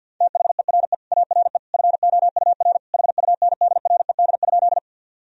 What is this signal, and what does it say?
Morse code, heard as a keyed tone, about 46 wpm, 700 Hz. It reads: THERE ARE FOUR HUNDRED?
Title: Sentences from Top 1000 Words - Review Part 2 - 50wpm